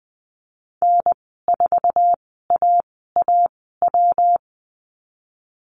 Morse code, heard as a keyed tone, about 20 wpm, spaced as for 15 wpm, 700 Hz. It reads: N4AAW
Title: 128 Lesson 21g - Number 6 - Call Signs 20wpm